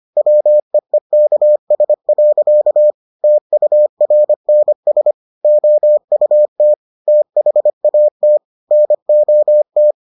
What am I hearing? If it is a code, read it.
Morse code, heard as a keyed tone, about 25 wpm, 600 Hz. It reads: WEEKS. TURNS OUT THAT NOT